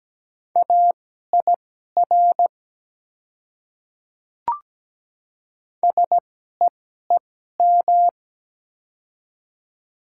Morse code, heard as a keyed tone, about 17 wpm, spaced as for 12 wpm, 700 Hz. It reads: AIR SEEM
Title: 2x Character Spacing - Top 300 Words - 17wpm